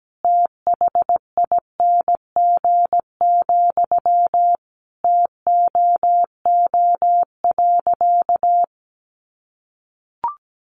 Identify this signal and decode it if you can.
Morse code, heard as a keyed tone, about 17 wpm, 700 Hz. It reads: THING, TOO.
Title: Sentences from Top 200 Words 17wpm